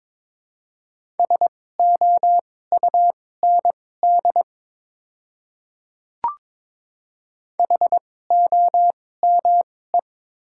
Morse code, heard as a keyed tone, about 22 wpm, spaced as for 16 wpm, 700 Hz. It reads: SOUND HOME
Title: 2x Character Spacing - Top 300 Words - 22wpm